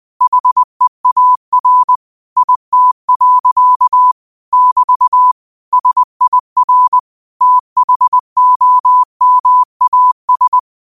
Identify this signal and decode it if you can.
Morse code, heard as a keyed tone, about 20 wpm, 1000 Hz. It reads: HEAR IT. <BT> SIR THOMAS